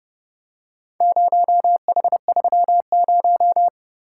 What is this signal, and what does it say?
Morse code, heard as a keyed tone, about 28 wpm, 700 Hz. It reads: 0H30